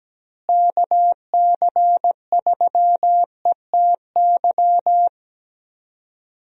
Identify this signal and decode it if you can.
Morse code, heard as a keyed tone, about 17 wpm, 700 Hz. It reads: KC3ETY
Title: USA Call Signs - 6 Characters - Repeated 2x - 17wpm